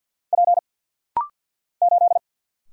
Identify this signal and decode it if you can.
Morse code, heard as a keyed tone, about 49 wpm, 700 Hz. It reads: P 8
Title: Single Letter-Number - Rapid-Fire 50wpm